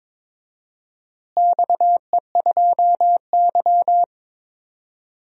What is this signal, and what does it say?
Morse code, heard as a keyed tone, about 22 wpm, 700 Hz. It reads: XE2Y
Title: DXCC-100 Call Signs - 4 Characters - Repeated 2x - 22wpm